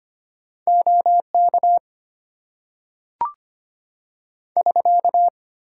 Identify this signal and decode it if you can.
Morse code, heard as a keyed tone, about 25 wpm, 700 Hz. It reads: OK <SK>
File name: Top 90 QSO Elements - 8x Character Spacing - 25wpm